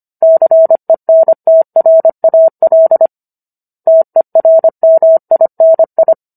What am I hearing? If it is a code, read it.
Morse code, heard as a keyed tone, about 25 wpm, 650 Hz. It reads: CENTRAL TERMINI
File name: Book One / 16-Chapter 16 - 25 WPM